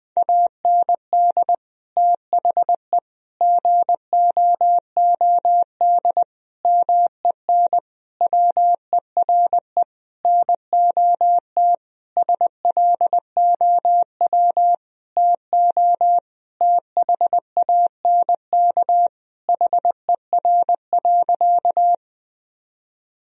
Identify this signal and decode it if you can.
Morse code, heard as a keyed tone, about 20 wpm, 700 Hz. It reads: AND THE GOOD MEN WERE NOT SLOW TO THANK HER.
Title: Sentences from Top 1000 Words - Review Part 2 - 20wpm